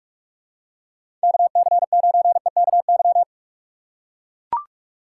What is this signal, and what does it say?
Morse code, heard as a keyed tone, about 44 wpm, 700 Hz. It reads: KC9EKY
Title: USA Call Signs - 6 Characters - 6x Character Spacing - 45wpm